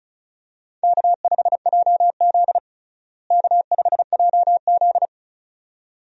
Morse code, heard as a keyed tone, about 36 wpm, 700 Hz. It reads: K5JZ K5JZ
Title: USA Call Signs - 4 Characters - Repeated 2x - 35wpm